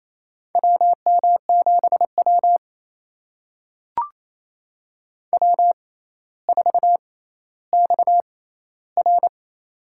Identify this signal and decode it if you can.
Morse code, heard as a keyed tone, about 28 wpm, spaced as for 14 wpm, 700 Hz. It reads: WM7W W4XR
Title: USA Call Signs - 4 Characters - 6x Character Spacing - 28wpm